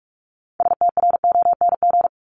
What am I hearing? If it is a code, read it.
Morse code, heard as a keyed tone, about 44 wpm, 700 Hz. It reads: STRONG
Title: LY Suffix - Top 100 Words with Base Words 45wpm